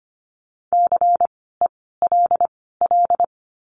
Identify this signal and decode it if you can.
Morse code, heard as a keyed tone, about 25 wpm, spaced as for 16 wpm, 700 Hz. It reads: CELL
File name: course / 25:10wpm / 094 Lesson 16f - Letter c - Words 25wpm